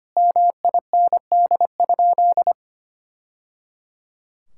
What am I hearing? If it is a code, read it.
Morse code, heard as a keyed tone, about 25 wpm, 700 Hz. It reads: MIND?
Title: Sentences from Top 1000 Words - Review Part 6 - 25wpm